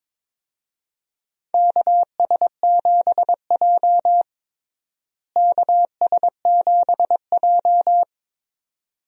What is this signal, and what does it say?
Morse code, heard as a keyed tone, about 21 wpm, 700 Hz. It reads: KS7J KS7J
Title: USA Call Signs - 4 Characters - Repeated 2x - 22wpm